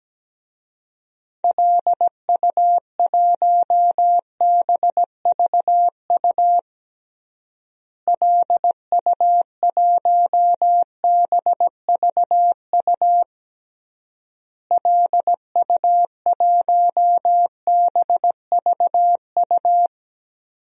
Morse code, heard as a keyed tone, about 17 wpm, 700 Hz. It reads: LU1BVU LU1BVU LU1BVU
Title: DXCC-100 Call Signs - 6 Characters - Repeated 3x - 17wpm